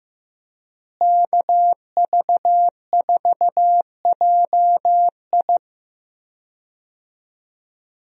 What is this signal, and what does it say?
Morse code, heard as a keyed tone, about 15 wpm, 700 Hz. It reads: KV4JI
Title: USA Call Signs - 5 Characters - Repeated 3x - 15wpm